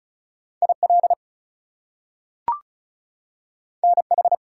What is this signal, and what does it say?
Morse code, heard as a keyed tone, about 36 wpm, 700 Hz. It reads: IL NH
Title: US States and Canadian Provinces 35wpm